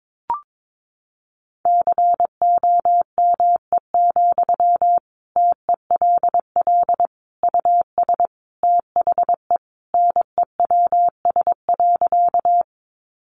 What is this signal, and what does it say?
Morse code, heard as a keyed tone, about 22 wpm, 700 Hz. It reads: COME, TELL US THE NEWS.
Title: Sentences from Top 800 Words 22wpm